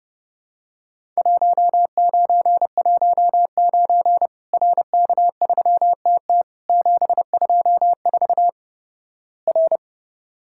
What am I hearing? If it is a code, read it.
Morse code, heard as a keyed tone, about 32 wpm, 700 Hz. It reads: 1919 RK3TT 724 R
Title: Worked All Europe DX Contest - QTC Exchange - 30wpm